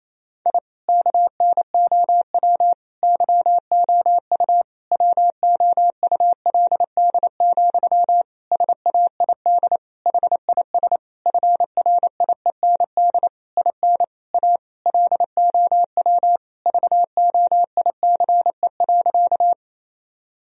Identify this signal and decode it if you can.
Morse code, heard as a keyed tone, about 28 wpm, 700 Hz. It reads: I KNOW YOU WOULD, SAID HIS FRIEND IN A LOW VOICE.
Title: Sentences from Top 1000 Words - Review Part 1 - 28wpm